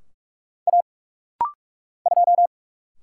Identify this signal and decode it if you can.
Morse code, heard as a keyed tone, about 46 wpm, 700 Hz. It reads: A 2
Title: Single Letter-Number Alt - Rapid-Fire 45wpm